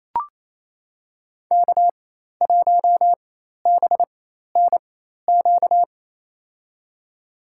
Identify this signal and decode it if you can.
Morse code, heard as a keyed tone, about 28 wpm, spaced as for 13 wpm, 700 Hz. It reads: K1BNQ
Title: USA Call Signs - 5 Characters - 4x Character Spacing - 28wpm